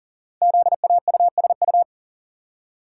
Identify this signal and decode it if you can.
Morse code, heard as a keyed tone, about 40 wpm, 700 Hz. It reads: ZAUSU
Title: Code Groups - 5 Letters - 40wpm